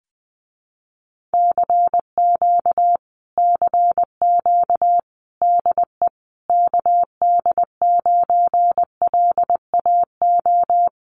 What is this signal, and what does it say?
Morse code, heard as a keyed tone, about 20 wpm, 700 Hz. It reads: CQ CQ DE KD9LAO
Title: Common QSO Phrases Encore 20wpm